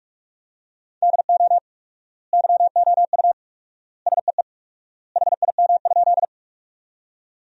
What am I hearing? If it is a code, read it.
Morse code, heard as a keyed tone, about 46 wpm, 700 Hz. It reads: DO YOU SEE HIM?
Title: 4x Word-Spacing - Sentences from Top 100 Words - 45wpm